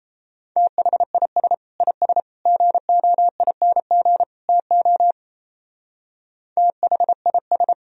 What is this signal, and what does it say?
Morse code, heard as a keyed tone, about 35 wpm, 700 Hz. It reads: THIS IS GOING TO THIS